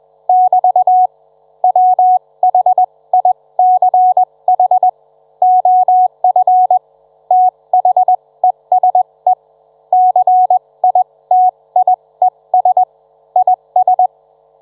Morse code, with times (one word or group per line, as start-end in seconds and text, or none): 0.3-1.1 <BT>
1.6-4.9 WHICH
5.4-6.8 OF
7.3-9.3 THESE
9.9-12.8 CITIES
13.4-14.1 IS